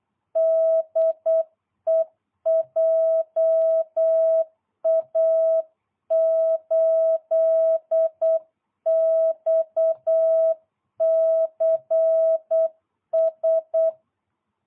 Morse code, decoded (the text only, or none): DE JA8XCS